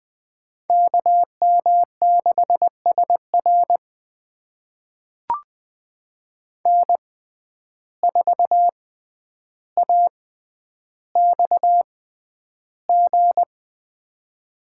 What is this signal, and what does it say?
Morse code, heard as a keyed tone, about 20 wpm, spaced as for 10 wpm, 700 Hz. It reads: KM6SR N4AXG
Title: USA Call Signs - 5 Characters - 6x Character Spacing - 20wpm